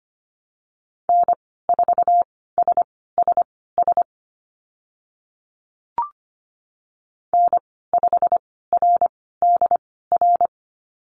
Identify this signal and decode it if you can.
Morse code, heard as a keyed tone, about 25 wpm, spaced as for 16 wpm, 700 Hz. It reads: N4SSS N5RDR
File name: Lesson 15g - Number 5 - Call Signs 25wpm